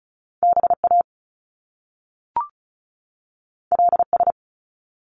Morse code, heard as a keyed tone, about 36 wpm, 700 Hz. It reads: BA LS
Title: Top Two Letter Combinations in Words 35wpm